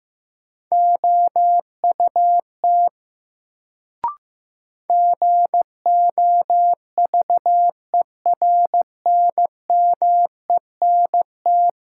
Morse code, heard as a keyed tone, about 15 wpm, 700 Hz. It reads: OUT GOVERNMENT
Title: Sets of 2 Words - New Word Drill - 15wpm